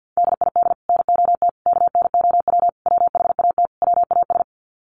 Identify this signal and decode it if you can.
Morse code, heard as a keyed tone, about 50 wpm, 700 Hz. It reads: DID NOT KNOW WHAT WAS
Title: Sets of 5 Words Encore2 50wpm